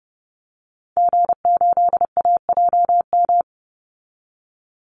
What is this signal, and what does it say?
Morse code, heard as a keyed tone, about 30 wpm, 700 Hz. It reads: G8AJM
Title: Call Signs Repeated 2x - 30wpm